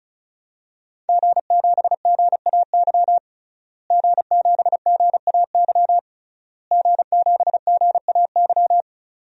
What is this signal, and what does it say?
Morse code, heard as a keyed tone, about 34 wpm, 700 Hz. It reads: G7GAY G7GAY G7GAY
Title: DXCC-100 Call Signs - 5 Characters - Repeated 3x - 35wpm